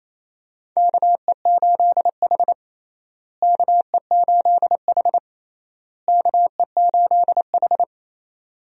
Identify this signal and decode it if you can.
Morse code, heard as a keyed tone, about 28 wpm, 700 Hz. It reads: KE8H KE8H KE8H
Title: USA Call Signs - 4 Characters - Repeated 3x - 28wpm